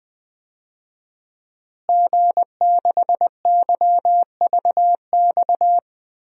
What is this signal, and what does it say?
Morse code, heard as a keyed tone, about 20 wpm, 700 Hz. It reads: G6YVX
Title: DXCC-100 Call Signs - 5 Characters - Repeated 3x - 20wpm